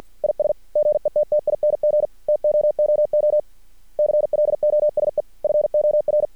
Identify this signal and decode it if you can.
Morse code, heard as a keyed tone, about 50 wpm, 600 Hz. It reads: IS GETTING TOOO CLOSE FOR
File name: archived / CRSnet